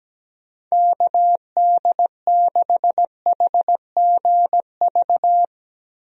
Morse code, heard as a keyed tone, about 17 wpm, 700 Hz. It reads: KD6HGV
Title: USA Call Signs - 6 Characters - 17wpm